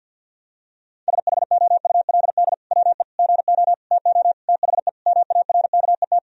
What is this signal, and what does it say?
Morse code, heard as a keyed tone, about 50 wpm, 700 Hz. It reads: SHOULD WE GO TO THE MARKET
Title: Sentences from Top 1000 Words - Review Part 3 - 50wpm